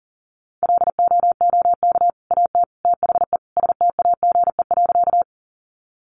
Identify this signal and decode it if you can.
Morse code, heard as a keyed tone, about 40 wpm, 700 Hz. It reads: LOOK AT THE STAGE.
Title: Sentences from Top 1200 Words 40wpm